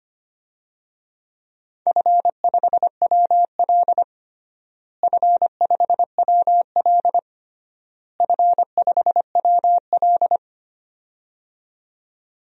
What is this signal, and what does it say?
Morse code, heard as a keyed tone, about 25 wpm, 700 Hz. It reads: F5WL F5WL F5WL
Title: DXCC-100 Call Signs - 4 Characters - Repeated 3x - 25wpm